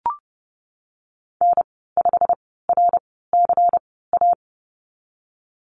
Morse code, heard as a keyed tone, about 30 wpm, spaced as for 17 wpm, 700 Hz. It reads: N5RCA
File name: Lesson 19g - Letter w - Call Signs 30wpm